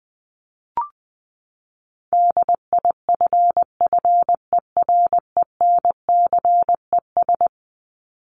Morse code, heard as a keyed tone, about 20 wpm, 700 Hz. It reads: DIFFERENCES
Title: ES:S Suffix - Top 100 Words with Base Words 20wpm